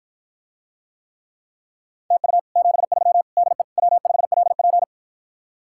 Morse code, heard as a keyed tone, about 54 wpm, 700 Hz. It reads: TU 73 DE W5LP